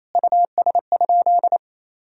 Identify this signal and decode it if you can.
Morse code, heard as a keyed tone, about 28 wpm, 700 Hz. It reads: US?
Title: Sentences from Top 1000 Words - Review Part 2 - 28wpm